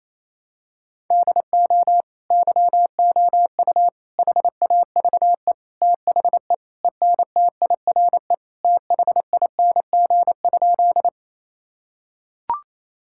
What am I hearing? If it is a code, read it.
Morse code, heard as a keyed tone, about 28 wpm, 700 Hz. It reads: DO YOU HAVE THE ENTIRE THING?
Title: Sentences from Top 1000 Words - Review Part 5 - 28wpm